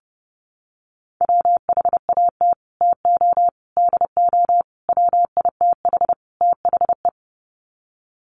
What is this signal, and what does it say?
Morse code, heard as a keyed tone, about 30 wpm, 700 Hz. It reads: WHAT TO DO WITH THE
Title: Sets of 5 Words Encore2 30wpm